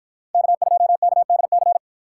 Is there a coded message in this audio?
Yes: Morse code from 0.3 to 1.8 s, heard as a keyed tone, about 52 wpm, 700 Hz. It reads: K2KDC